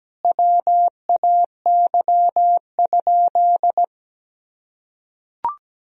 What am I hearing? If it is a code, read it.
Morse code, heard as a keyed tone, about 17 wpm, 700 Hz. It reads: WAY?
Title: Sentences from Top 1000 Words - Review Part 1 - 17wpm